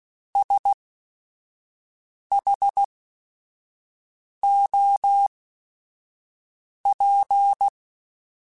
Morse code, heard as a keyed tone, about 16 wpm, spaced as for 5 wpm, 800 Hz. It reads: SHOP
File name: Weather Sets / Weather Sets-mp3s / Weather 4 24 @5